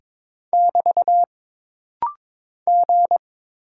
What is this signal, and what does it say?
Morse code, heard as a keyed tone, about 22 wpm, 700 Hz. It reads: <BT> G
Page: This rapid-fire timing is geared to push your ICR (Instant Character Recognition) skills.